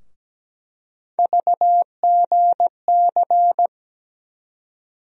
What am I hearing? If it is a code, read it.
Morse code, heard as a keyed tone, about 17 wpm, 700 Hz. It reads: VGC
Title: USA Call Sign Postfixes - 3 Letters 17wpm